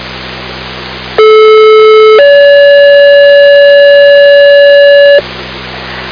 Two-tone page, heard as a one-second tone, then a three-second tone, about 430 then 570 Hz, with noise. It is heard from 1.2 to 5.2 s.